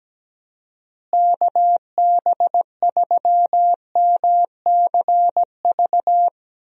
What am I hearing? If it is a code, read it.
Morse code, heard as a keyed tone, about 17 wpm, 700 Hz. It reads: KB3MCV
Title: USA Call Signs - 6 Characters - Repeated 2x - 17wpm